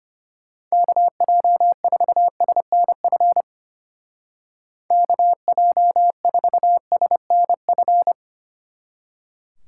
Morse code, heard as a keyed tone, about 30 wpm, 700 Hz. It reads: KJ4SNF KJ4SNF
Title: Speed Racing - USA 2x3 Call Signs 25wpm